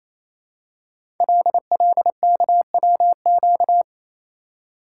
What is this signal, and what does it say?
Morse code, heard as a keyed tone, about 29 wpm, 700 Hz. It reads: LLKWQ
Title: Code Groups - 5 Letter:Numbers - 28wpm